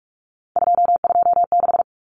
Morse code, heard as a keyed tone, about 44 wpm, 700 Hz. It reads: 226
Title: Numbers - 3-Digit 45wpm